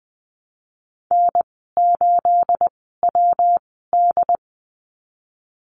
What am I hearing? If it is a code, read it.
Morse code, heard as a keyed tone, about 20 wpm, spaced as for 15 wpm, 700 Hz. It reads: N8WD
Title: 238 Lesson 38g - Letter q - Call Signs 20wpm